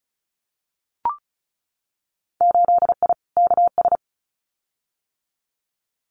Morse code, heard as a keyed tone, about 34 wpm, 700 Hz. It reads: 8I KS
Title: Winter Field Day Exchanges Encore 35wpm